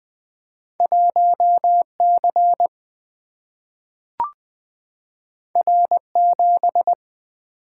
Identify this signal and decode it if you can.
Morse code, heard as a keyed tone, about 20 wpm, 700 Hz. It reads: 1C R7
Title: Single-Letter Single-Number 20wpm